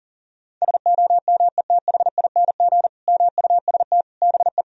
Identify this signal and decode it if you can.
Morse code, heard as a keyed tone, about 40 wpm, 700 Hz. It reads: SOMETHING MUST BE
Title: Sentences from Top 1000 Words - Review Part 1 - 40wpm